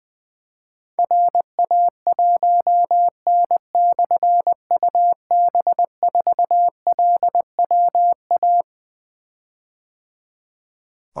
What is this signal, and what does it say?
Morse code, heard as a keyed tone, about 20 wpm, 700 Hz. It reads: RA1N/UB4LWA
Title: Difficult Call Signs 20wpm